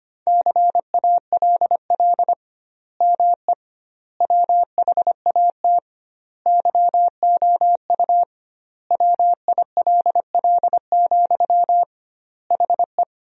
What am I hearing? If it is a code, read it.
Morse code, heard as a keyed tone, about 25 wpm, 700 Hz. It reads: CALL ME WHAT YOU WILL, HE
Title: 2x Word-Spacing - Sentences from Top 200 Words - 25wpm